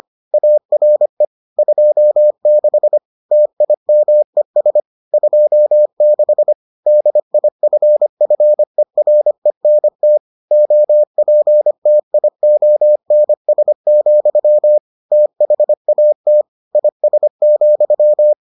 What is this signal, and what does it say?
Morse code, heard as a keyed tone, about 25 wpm, 600 Hz. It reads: ARE 26 TIMES 26 DIFFERENT OPTIONS, THAT IS,